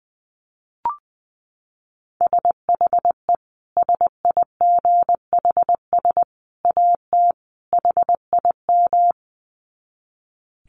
Morse code, heard as a keyed tone, about 20 wpm, 700 Hz. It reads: SHE SIGHS AT HIM
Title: Dit Challenge 20wpm